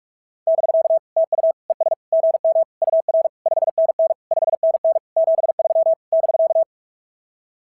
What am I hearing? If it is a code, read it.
Morse code, heard as a keyed tone, about 44 wpm, 690 Hz. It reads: <BK> TU ES GM UR 5NN 5NN 73 <BK>
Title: SOTA QSO Practice 45wpm